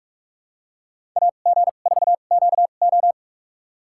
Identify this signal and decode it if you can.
Morse code, heard as a keyed tone, about 42 wpm, spaced as for 31 wpm, 700 Hz. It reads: AG4QO
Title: USA Call Signs - 5 Characters - 2x Character Spacing - 45wpm